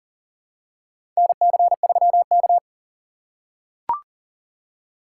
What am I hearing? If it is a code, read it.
Morse code, heard as a keyed tone, about 40 wpm, 700 Hz. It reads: NC3K
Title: USA Call Signs - 4 Characters - 2x Character Spacing - 40wpm